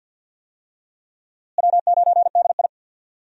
Morse code, heard as a keyed tone, about 52 wpm, 700 Hz. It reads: W9DI
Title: USA Call Signs - 4 Characters - Repeated 3x - 50wpm